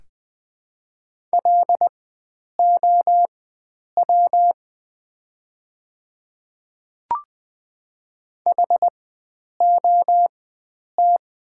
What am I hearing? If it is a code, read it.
Morse code, heard as a keyed tone, about 20 wpm, spaced as for 9 wpm, 700 Hz. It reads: LOW HOT
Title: 4x Character Spacing - Top 500 Words - 20wpm